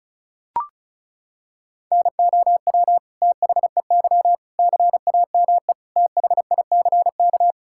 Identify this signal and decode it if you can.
Morse code, heard as a keyed tone, about 36 wpm, 700 Hz. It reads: NOW THEY CAME THICK
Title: Sentences from Top 1600 Words 35wpm